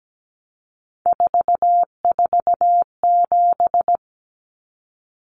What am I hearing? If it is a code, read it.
Morse code, heard as a keyed tone, about 17 wpm, 700 Hz. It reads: 447
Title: Common Cut Numbers - 3-Digit 17wpm